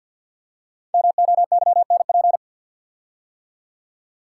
Morse code, heard as a keyed tone, about 47 wpm, 700 Hz. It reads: MOYNP
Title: Code Groups - 5 Letter:Numbers - 50wpm